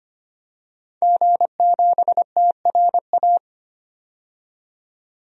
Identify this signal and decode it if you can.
Morse code, heard as a keyed tone, about 25 wpm, 700 Hz. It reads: G7TRA
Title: DXCC-100 Call Signs - 5 Characters - 25wpm